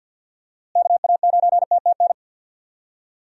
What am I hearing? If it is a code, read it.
Morse code, heard as a keyed tone, about 52 wpm, 700 Hz. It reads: KA9TTN
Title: USA Call Signs - 6 Characters - Repeated 2x - 50wpm